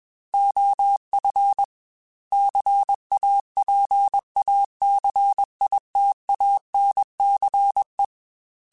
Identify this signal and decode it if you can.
Morse code, heard as a keyed tone, about 21 wpm, 800 Hz. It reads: OF CAPACITANCE
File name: Technical Subjects / Technical Subjects-mp3s / XC @20